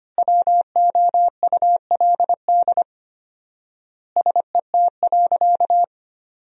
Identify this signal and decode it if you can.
Morse code, heard as a keyed tone, about 25 wpm, 700 Hz. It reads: WOULD SET.
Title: 4x Word-Spacing - Sentences from Top 400 Words - 25wpm